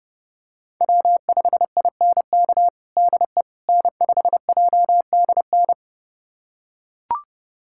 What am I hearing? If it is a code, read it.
Morse code, heard as a keyed tone, about 30 wpm, 700 Hz. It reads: W5INK DE N5JDN
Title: USA Callsign DE Callsign - 5 Characters - 30wpm